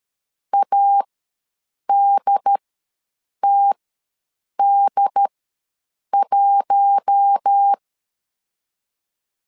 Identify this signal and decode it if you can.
Morse code, heard as a keyed tone, about 13 wpm, spaced as for 7 wpm, 800 Hz. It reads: ADTD1
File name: LEZIONI / 5 LBVC / 5 LBV ws 12wpm 3015